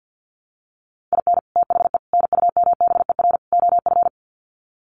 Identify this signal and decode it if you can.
Morse code, heard as a keyed tone, about 50 wpm, 700 Hz. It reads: IN THE NUMBER OF
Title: Sets of 4 Words 50wpm